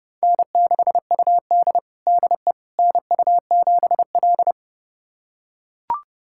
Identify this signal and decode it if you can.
Morse code, heard as a keyed tone, about 32 wpm, 700 Hz. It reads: N6UD DE NU7L
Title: USA Callsign DE Callsign - 4 Characters - 30wpm